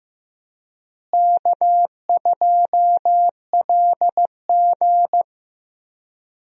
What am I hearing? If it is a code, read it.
Morse code, heard as a keyed tone, about 15 wpm, 700 Hz. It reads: K2LG
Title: USA Call Signs - 4 Characters - Repeated 2x - 15wpm